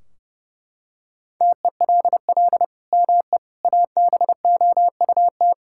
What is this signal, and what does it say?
Morse code, heard as a keyed tone, about 30 wpm, 700 Hz. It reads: TELL ME ABOUT
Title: Sentences from Top 1600 Words 30wpm